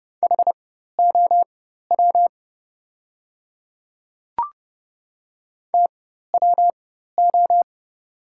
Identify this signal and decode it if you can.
Morse code, heard as a keyed tone, about 30 wpm, spaced as for 14 wpm, 700 Hz. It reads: HOW TWO
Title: 4x Character Spacing - Top 100 Words - 30wpm